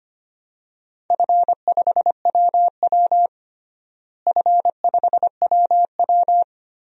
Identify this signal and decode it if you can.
Morse code, heard as a keyed tone, about 25 wpm, 700 Hz. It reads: F5WW F5WW
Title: DXCC-100 Call Signs - 4 Characters - Repeated 3x - 25wpm